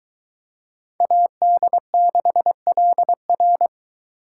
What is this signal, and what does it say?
Morse code, heard as a keyed tone, about 22 wpm, 700 Hz. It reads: AD6LR